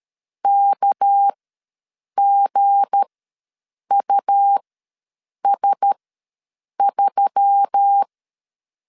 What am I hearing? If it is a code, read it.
Morse code, heard as a keyed tone, about 13 wpm, spaced as for 7 wpm, 800 Hz. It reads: KGUS3
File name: 7 GF9Z ws 12wpm